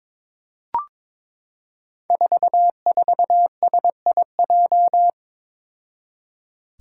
Morse code, heard as a keyed tone, about 22 wpm, 700 Hz. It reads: 44SIJ
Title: Five Challenging Letter-Number Combinations 22wpm